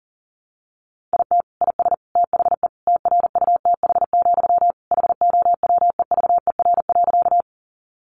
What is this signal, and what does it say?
Morse code, heard as a keyed tone, about 40 wpm, 700 Hz. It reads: IT IS THE TRUTH, HOWEVER.
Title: Sentences from Top 800 Words 40wpm